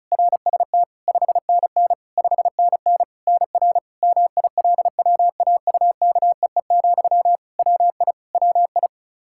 Morse code, heard as a keyed tone, about 36 wpm, 700 Hz. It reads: RST 5NN 5NN NR MILWAUKEE, WI WI
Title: Practice QSO - Exchange 1 - Beginning - 35wpm